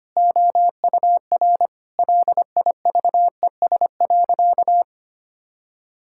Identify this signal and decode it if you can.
Morse code, heard as a keyed tone, about 25 wpm, 700 Hz. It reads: OUR LIVES.
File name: practice / 25wpm / Sentences with 2-Word Phrases - 25wpm